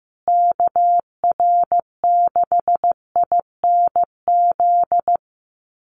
Morse code, heard as a keyed tone, about 15 wpm, 700 Hz. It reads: KR6INZ
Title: USA Call Signs Repeated 2x 15wpm